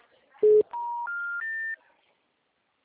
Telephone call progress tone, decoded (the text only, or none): special information tone